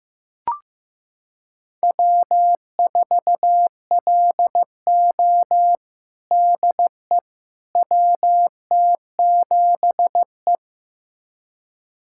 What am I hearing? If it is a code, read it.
Morse code, heard as a keyed tone, about 15 wpm, 700 Hz. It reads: W4LO DE WT7E
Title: USA Callsign DE Callsign - 4 Characters - 15wpm